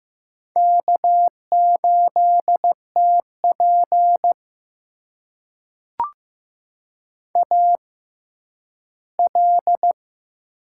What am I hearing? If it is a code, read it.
Morse code, heard as a keyed tone, about 15 wpm, spaced as for 10 wpm, 700 Hz. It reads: K8TP AL
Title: USA Call Signs - 6x Character Spacing - 15wpm